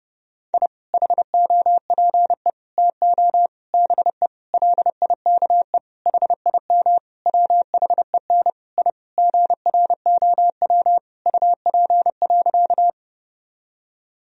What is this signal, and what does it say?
Morse code, heard as a keyed tone, about 32 wpm, 700 Hz. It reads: I HOPE TO BE LIKE HIM WHEN I GROW UP.
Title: Sentences from Top 1000 Words - Review Part 1 - 30wpm